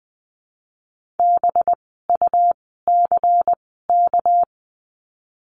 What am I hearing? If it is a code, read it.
Morse code, heard as a keyed tone, about 20 wpm, spaced as for 15 wpm, 700 Hz. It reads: BUCK
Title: 199 Lesson 32f - Letter k - Words 20wpm